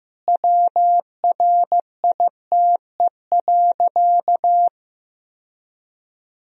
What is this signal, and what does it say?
Morse code, heard as a keyed tone, about 14 wpm, 700 Hz. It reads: WRITE.
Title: Sentences from Top 1000 Words - Review Part 1 - 15wpm